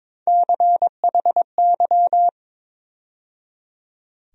Morse code, heard as a keyed tone, about 22 wpm, 700 Hz. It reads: CHY
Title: USA Call Sign Postfixes - 3 Letters 22wpm